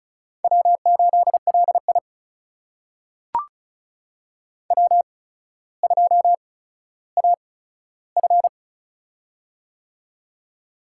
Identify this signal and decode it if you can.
Morse code, heard as a keyed tone, about 36 wpm, spaced as for 15 wpm, 700 Hz. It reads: W8LI W2AF
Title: USA Call Signs - 4 Characters - 8x Character Spacing - 35wpm